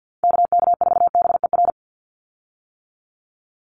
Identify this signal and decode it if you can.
Morse code, heard as a keyed tone, about 50 wpm, 700 Hz. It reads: KK4BER
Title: USA Call Signs 50wpm